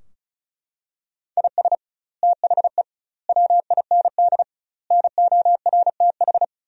Morse code, heard as a keyed tone, about 36 wpm, 700 Hz. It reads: IS THE WIND NORTH
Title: 2x Word-Spacing - Sentences from Top 400 Words - 35wpm